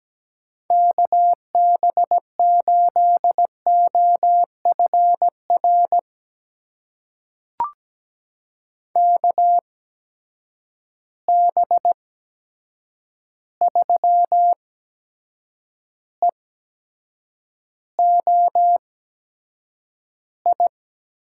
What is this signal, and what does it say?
Morse code, heard as a keyed tone, about 17 wpm, spaced as for 7 wpm, 700 Hz. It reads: KB8OFR KB3EOI
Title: USA Call Signs - 6 Characters - 8x Character Spacing - 17wpm